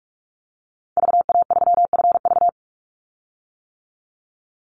Morse code, heard as a keyed tone, about 44 wpm, 700 Hz. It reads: VA3FV
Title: Call Signs - 45wpm